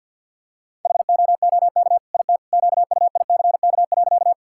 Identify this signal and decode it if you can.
Morse code, heard as a keyed tone, about 50 wpm, 700 Hz. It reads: HOOK IT QUICK.
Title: Sentences from Top 1600 Words 50wpm